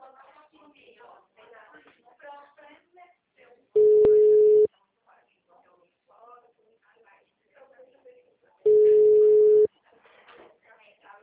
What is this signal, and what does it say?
Ringback tone (Europe, 425 Hz), with speech.